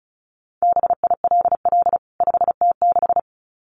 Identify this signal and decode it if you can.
Morse code, heard as a keyed tone, about 34 wpm, 700 Hz. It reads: BILL 5T6
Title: CWT Exchange 35wpm